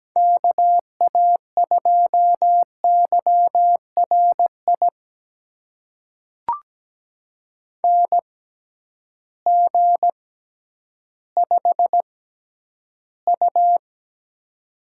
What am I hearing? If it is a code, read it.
Morse code, heard as a keyed tone, about 17 wpm, spaced as for 10 wpm, 700 Hz. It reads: KA2YRI NG5U
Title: USA Call Signs - 6x Character Spacing - 17wpm